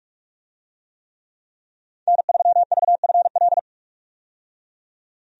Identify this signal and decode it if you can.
Morse code, heard as a keyed tone, about 46 wpm, 700 Hz. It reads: N3VFL
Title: USA Call Signs - 5 Characters - 45wpm